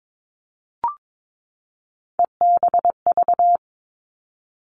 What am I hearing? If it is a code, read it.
Morse code, heard as a keyed tone, about 22 wpm, 700 Hz. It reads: EBV
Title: All Cut Numbers - 3-Digit 22wpm